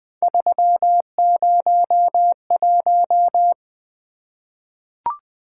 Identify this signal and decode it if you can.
Morse code, heard as a keyed tone, about 20 wpm, 700 Hz. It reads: 301
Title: Three Letter-Number Combinations - 20wpm